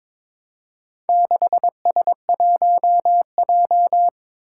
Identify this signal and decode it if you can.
Morse code, heard as a keyed tone, about 22 wpm, 700 Hz. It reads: 6S1J
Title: Four Challenging Letter-Number Combinations 22wpm